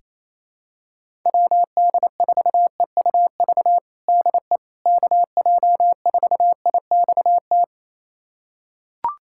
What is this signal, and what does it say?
Morse code, heard as a keyed tone, about 28 wpm, 700 Hz. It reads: WD4EUV DE KJ4IXT
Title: USA Callsign DE Callsign - 6 Characters - 28wpm